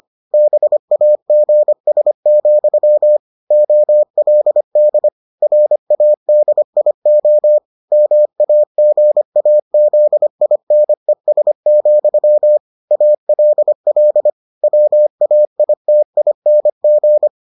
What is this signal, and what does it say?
Morse code, heard as a keyed tone, about 25 wpm, 600 Hz. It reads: BAGS, OLD RADIO MAGAZINES, ALL WAITING